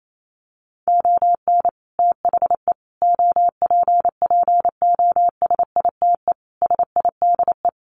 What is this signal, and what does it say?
Morse code, heard as a keyed tone, about 28 wpm, 700 Hz. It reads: ON THE OPPOSITE SIDE
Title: Sets of 5 Words Encore2 28wpm